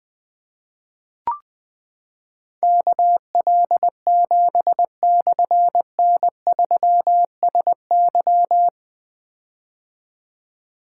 Morse code, heard as a keyed tone, about 20 wpm, 700 Hz. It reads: KL7/N3SY
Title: Difficult Call Signs 20wpm